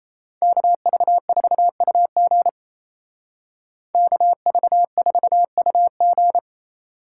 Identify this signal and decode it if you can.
Morse code, heard as a keyed tone, about 36 wpm, 700 Hz. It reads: KV4UG KV4UG